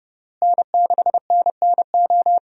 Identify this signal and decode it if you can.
Morse code, heard as a keyed tone, about 29 wpm, 700 Hz. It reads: N6NNO
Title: USA Call Signs - 5 Characters - Repeated 3x - 30wpm